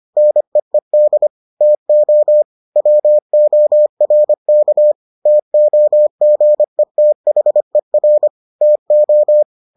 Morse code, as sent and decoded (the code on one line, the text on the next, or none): -. . -.. / - --- / .-- --- .-. -.- / - --- --. - .... .-. / - ---
NEED TO WORK TOGETHER TO